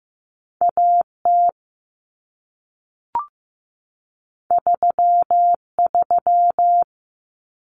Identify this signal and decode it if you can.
Morse code, heard as a keyed tone, about 15 wpm, 700 Hz. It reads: AT 33
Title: Common Cut Numbers - 2-Digit 15wpm